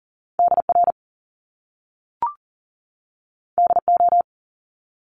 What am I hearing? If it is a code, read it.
Morse code, heard as a keyed tone, about 40 wpm, 700 Hz. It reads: DR DO
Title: Top Two Letter Combinations in Words 40wpm